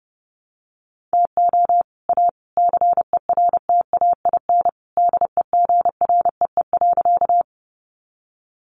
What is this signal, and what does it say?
Morse code, heard as a keyed tone, about 30 wpm, 700 Hz. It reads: TO A CERTAIN DEGREE.
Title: Sentences from Top 1500 Words 30wpm